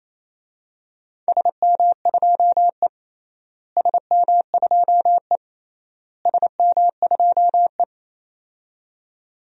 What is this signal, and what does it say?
Morse code, heard as a keyed tone, about 28 wpm, 700 Hz. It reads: SM2E SM2E SM2E
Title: DXCC-100 Call Signs - 4 Characters - Repeated 3x - 28wpm